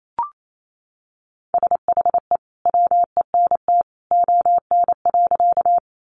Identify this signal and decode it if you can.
Morse code, heard as a keyed tone, about 28 wpm, 700 Hz. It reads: SHE WENT ON.
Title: Sentences from Top 200 Words 28wpm